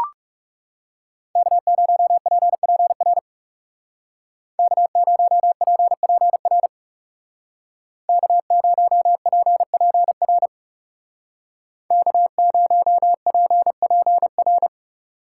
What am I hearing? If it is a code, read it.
Morse code, heard as a keyed tone, about 36 wpm, 700 Hz. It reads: K0PPR K0PPR K0PPR K0PPR
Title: Speed Racing - USA 1x3 Call Signs 30wpm